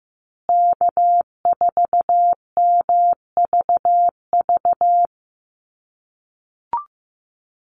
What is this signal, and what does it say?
Morse code, heard as a keyed tone, about 15 wpm, 700 Hz. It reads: K4MVV
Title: USA Call Signs Repeated 3x 15wpm